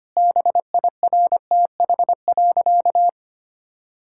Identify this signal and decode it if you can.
Morse code, heard as a keyed tone, about 25 wpm, 700 Hz. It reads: BIRTH.
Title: Sentences from Top 1000 Words - Review Part 5 - 25wpm